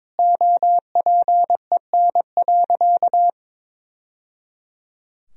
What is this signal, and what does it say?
Morse code, heard as a keyed tone, about 22 wpm, 700 Hz. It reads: OPEN.
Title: Sentences from Top 1000 Words - Review Part 3 - 22wpm